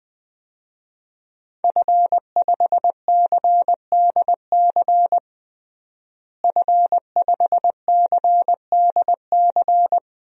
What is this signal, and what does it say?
Morse code, heard as a keyed tone, about 21 wpm, 700 Hz. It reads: F5CDC F5CDC
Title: DXCC-100 Call Signs - 5 Characters - Repeated 2x - 20wpm